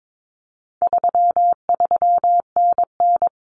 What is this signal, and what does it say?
Morse code, heard as a keyed tone, about 22 wpm, 700 Hz. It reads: 33NN
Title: Common Cut Numbers - 4-Digit 22wpm